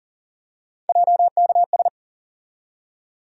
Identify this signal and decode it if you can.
Morse code, heard as a keyed tone, about 40 wpm, 700 Hz. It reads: JKS
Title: USA Call Sign Postfixes - 3 Letters 40wpm